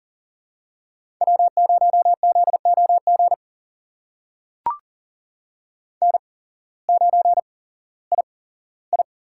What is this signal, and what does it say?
Morse code, heard as a keyed tone, about 40 wpm, spaced as for 19 wpm, 700 Hz. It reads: W0ZOG N9II
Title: USA Call Signs - 5 Characters - 8x Character Spacing - 40wpm